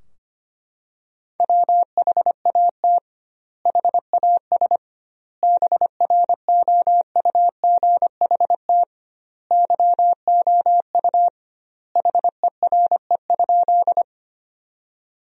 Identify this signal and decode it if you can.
Morse code, heard as a keyed tone, about 25 wpm, 700 Hz. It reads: WHAT HAS BROUGHT YOU HERE?
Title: 2x Word-Spacing - Sentences from Top 500 Words - 25wpm